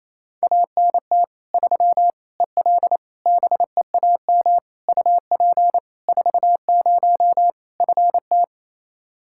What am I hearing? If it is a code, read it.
Morse code, heard as a keyed tone, about 28 wpm, 700 Hz. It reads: ANT 3 EL BEAM UP 40 FT
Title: K1USN SST QSO Practice 28wpm